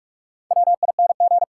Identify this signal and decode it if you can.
Morse code, heard as a keyed tone, about 46 wpm, 700 Hz. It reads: WING
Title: Four Letter Words - 45wpm